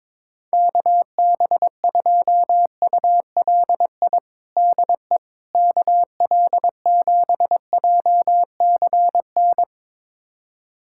Morse code, heard as a keyed tone, about 22 wpm, 700 Hz. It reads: KB2ULI DE KL7JCN
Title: USA Callsign DE Callsign 22wpm